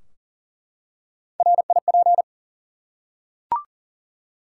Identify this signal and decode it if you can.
Morse code, heard as a keyed tone, about 40 wpm, 700 Hz. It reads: RIP
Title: Three Letter Words - 40wpm